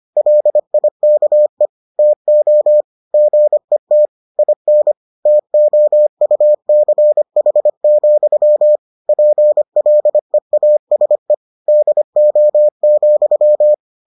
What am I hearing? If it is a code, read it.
Morse code, heard as a keyed tone, about 25 wpm, 600 Hz. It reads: LIKE TO GET IN TOUCH, PLEASE DO,